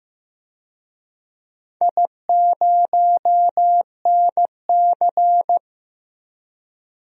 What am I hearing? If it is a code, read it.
Morse code, heard as a keyed tone, about 15 wpm, 700 Hz. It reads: I0NC
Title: DXCC-100 Call Signs - 4 Characters - 15wpm